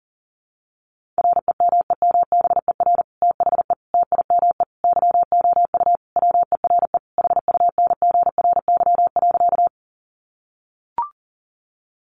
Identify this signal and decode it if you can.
Morse code, heard as a keyed tone, about 40 wpm, 700 Hz. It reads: REMEMBER THE TIME YOU WERE HUNGRY.
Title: Sentences from Top 1400 Words 40wpm